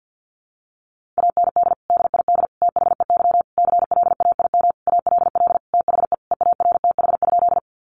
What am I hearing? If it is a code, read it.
Morse code, heard as a keyed tone, about 50 wpm, 700 Hz. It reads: AND DID THEY CLAIM ALL THE EARTH?
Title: Sentences from Top 1500 Words 50wpm